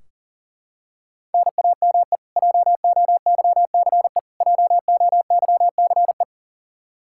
Morse code, heard as a keyed tone, about 40 wpm, 700 Hz. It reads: NAME JOYCE JOYCE
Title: Practice QSO - Exchange 1 - Beginning - 40wpm